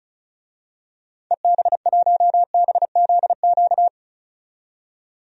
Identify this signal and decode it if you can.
Morse code, heard as a keyed tone, about 36 wpm, 700 Hz. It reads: EB1BZQ
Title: DXCC-100 Call Signs - 6 Characters - 35wpm